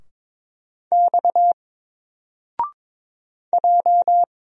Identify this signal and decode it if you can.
Morse code, heard as a keyed tone, about 22 wpm, 700 Hz. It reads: X J
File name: Single-Letter-Number-22wpm